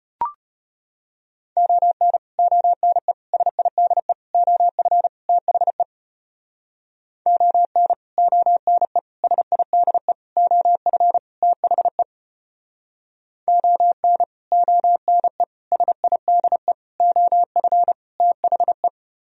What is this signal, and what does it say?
Morse code, heard as a keyed tone, about 32 wpm, 700 Hz. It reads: ON ONE SIDE OF THE ON ONE SIDE OF THE ON ONE SIDE OF THE